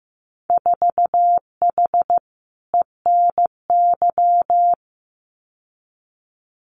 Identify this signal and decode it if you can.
Morse code, heard as a keyed tone, about 15 wpm, 700 Hz. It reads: 4H ENY
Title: Winter Field Day Exchanges Encore 15wpm